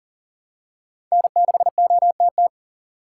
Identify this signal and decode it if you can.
Morse code, heard as a keyed tone, about 40 wpm, 700 Hz. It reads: N6OTT
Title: USA Call Signs - 5 Characters - 6x Character Spacing - 40wpm